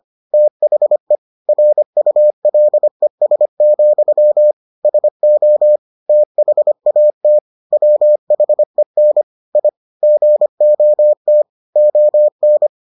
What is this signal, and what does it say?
Morse code, heard as a keyed tone, about 25 wpm, 600 Hz. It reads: THE RULES, SO THAT WHEN I GOT ON